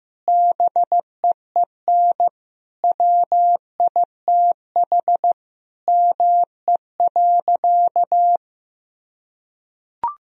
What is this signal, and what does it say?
Morse code, heard as a keyed tone, about 16 wpm, 700 Hz. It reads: BEEN WITH ME.
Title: Sentences from Top 1000 Words - Review Part 1 - 15wpm